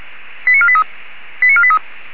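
Five-tone selective call (ZVEI), heard as quick runs of five tones, tones of about 70 ms. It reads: 89483 89483
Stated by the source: ZVEI five-tone call received by SDR.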